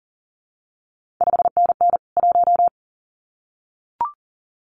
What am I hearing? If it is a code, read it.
Morse code, heard as a keyed tone, about 41 wpm, 700 Hz. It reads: 5NN 1
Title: CQ WW DX Contest Exchange Encore - 40wpm